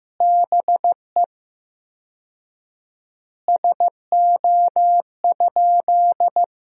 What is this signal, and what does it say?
Morse code, heard as a keyed tone, about 15 wpm, 700 Hz. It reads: BE SO?
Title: 4x Word-Spacing - Sentences from Top 300 Words - 15wpm